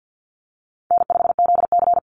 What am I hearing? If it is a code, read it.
Morse code, heard as a keyed tone, about 49 wpm, 700 Hz. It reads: N5ZC
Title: NA SOTA Chaser Callsigns 50wpm